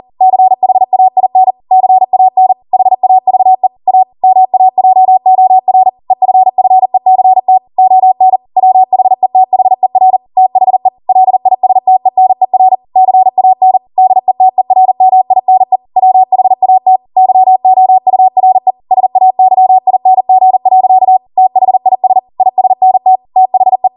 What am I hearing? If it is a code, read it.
Morse code, heard as a keyed tone, about 40 wpm, 750 Hz. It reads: CHAIN CAN HAVE A MAJOR EFFECT ON WHETHER THE LISTENER CAN DETERMINE WHAT YOURE SAYING. THIS ISNT THE